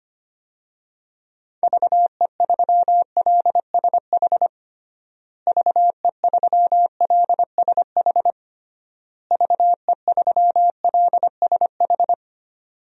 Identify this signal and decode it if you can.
Morse code, heard as a keyed tone, about 25 wpm, 700 Hz. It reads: VE3LSH VE3LSH VE3LSH
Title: DXCC-100 Call Signs - 6 Characters - Repeated 3x - 25wpm